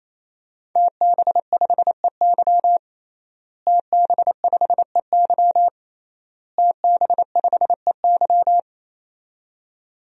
Morse code, heard as a keyed tone, about 27 wpm, 700 Hz. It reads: TB5EY TB5EY TB5EY
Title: DXCC-100 Call Signs - 5 Characters - Repeated 3x - 28wpm